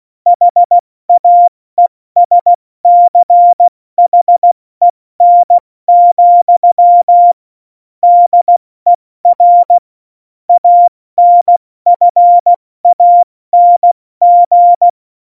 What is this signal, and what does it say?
Morse code, heard as a keyed tone, about 16 wpm, 700 Hz. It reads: HAESCHEN, DER ANFANG
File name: Morsekurs / Witze / Witz 13. November 2018, 70 F80